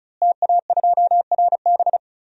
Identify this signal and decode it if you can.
Morse code, heard as a keyed tone, about 36 wpm, 700 Hz. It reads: TA2RB
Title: DXCC-100 Call Signs - 5 Characters - Repeated 2x - 35wpm